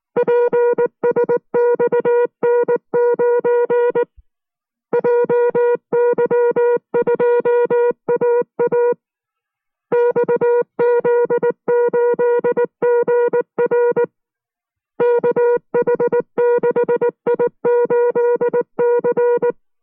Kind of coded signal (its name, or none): Morse code